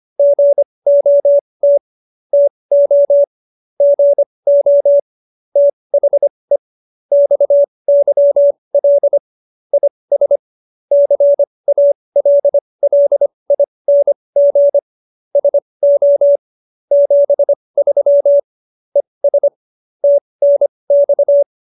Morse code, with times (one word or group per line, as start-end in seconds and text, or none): 0.2-1.8 GOT
2.3-3.2 TO
3.8-5.0 GO
5.6-6.6 THE
7.1-9.2 XYL
9.7-10.4 IS
10.9-14.8 CALLING
15.4-16.4 SO
16.9-18.4 73
19.0-19.5 ES
20.0-21.4 TNX